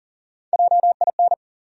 Morse code, heard as a keyed tone, about 40 wpm, 700 Hz. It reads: JIN
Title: USA Call Sign Postfixes - 3 Letters 40wpm